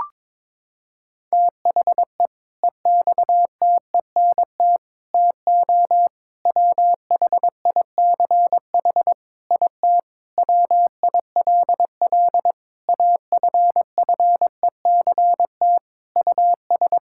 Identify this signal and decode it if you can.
Morse code, heard as a keyed tone, about 22 wpm, 700 Hz. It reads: THE EXTENT TO WHICH IT WILL AFFECT US
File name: practice / 22wpm / Sentences with 4-Word Phrases - 22wpm